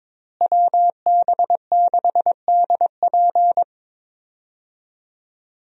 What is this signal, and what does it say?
Morse code, heard as a keyed tone, about 22 wpm, 700 Hz. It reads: WB6DP
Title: USA Call Signs - 5 Characters - Repeated 3x - 22wpm